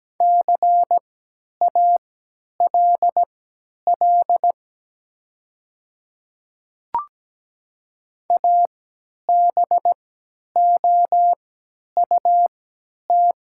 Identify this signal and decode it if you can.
Morse code, heard as a keyed tone, about 17 wpm, spaced as for 10 wpm, 700 Hz. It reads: CALL ABOUT
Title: 3x Character Spacing - Top 500 Words - 17wpm